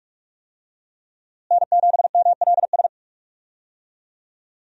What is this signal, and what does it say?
Morse code, heard as a keyed tone, about 42 wpm, 700 Hz. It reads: N7MLS
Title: USA Call Signs - 5 Characters - Repeated 2x - 45wpm